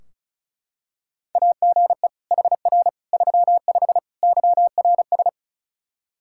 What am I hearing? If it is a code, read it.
Morse code, heard as a keyed tone, about 36 wpm, 700 Hz. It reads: AGE HR 35 YRS
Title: Practice QSO - Exchange 3 - More Info - 35wpm